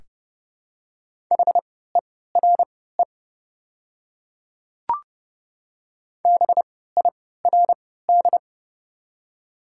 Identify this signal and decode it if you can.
Morse code, heard as a keyed tone, about 30 wpm, spaced as for 17 wpm, 690 Hz. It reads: HERE BIRD
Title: 3x Character Spacing - Top 500 Words - 30wpm